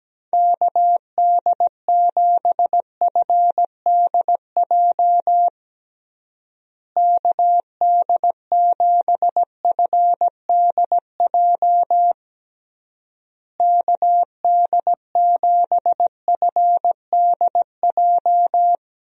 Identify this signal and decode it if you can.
Morse code, heard as a keyed tone, about 17 wpm, 700 Hz. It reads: KD7FDJ KD7FDJ KD7FDJ
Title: USA Call Signs - 6 Characters - Repeated 3x - 17wpm